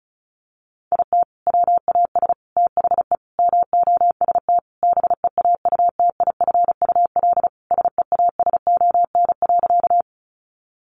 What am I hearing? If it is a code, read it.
Morse code, heard as a keyed tone, about 34 wpm, 700 Hz. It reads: IT WAS THE MOST BEAUTIFUL SEASON.
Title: Sentences from Top 1400 Words 35wpm